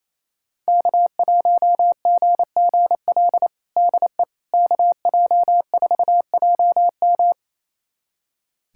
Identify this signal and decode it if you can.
Morse code, heard as a keyed tone, about 28 wpm, 700 Hz. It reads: K1GGL DE KJ4JM
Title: USA Callsign DE Callsign - 5 Characters - 28wpm